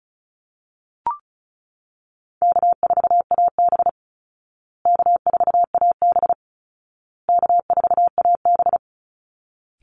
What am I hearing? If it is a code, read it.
Morse code, heard as a keyed tone, about 34 wpm, 700 Hz. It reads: K4AB K4AB K4AB
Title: Easy Call Signs Repeated 3x - 35wpm